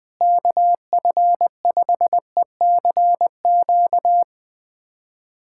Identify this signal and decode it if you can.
Morse code, heard as a keyed tone, about 20 wpm, 700 Hz. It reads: KF5ECQ
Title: USA Callsign DE Callsign - 6 Characters - 20wpm